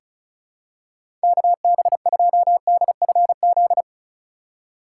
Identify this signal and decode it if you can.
Morse code, heard as a keyed tone, about 36 wpm, 700 Hz. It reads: KB2DFZ
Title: USA Call Signs - 6 Characters - 4x Character Spacing - 35wpm